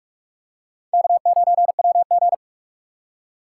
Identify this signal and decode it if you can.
Morse code, heard as a keyed tone, about 43 wpm, 700 Hz. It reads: K9WG
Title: USA Call Signs - 2x Character Spacing - 45wpm